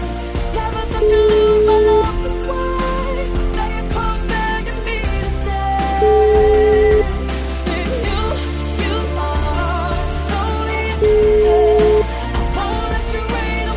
Ringback tone (Europe, 425 Hz), with noise.